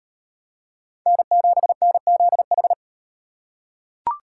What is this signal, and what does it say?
Morse code, heard as a keyed tone, about 36 wpm, 700 Hz. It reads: N7NZH